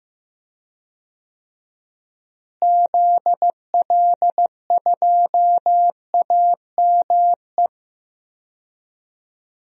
Morse code, heard as a keyed tone, about 15 wpm, 700 Hz. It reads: ZL2AME
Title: DXCC-100 Call Signs - 6 Characters - Repeated 2x - 15wpm